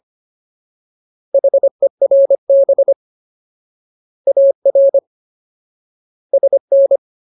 Morse code, heard as a keyed tone, about 25 wpm, 550 Hz. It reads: HERB AR SN
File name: Ultimate Word List / 25WPM Ultimate